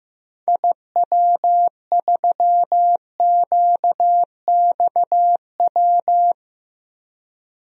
Morse code, heard as a keyed tone, about 15 wpm, 700 Hz. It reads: IW3QXW
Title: DXCC-100 Call Signs - 6 Characters - 15wpm